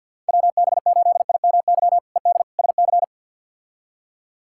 Morse code, heard as a keyed tone, about 46 wpm, 700 Hz. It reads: WB8IMY ED SC
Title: K1USN SST QSO Practice - 50wpm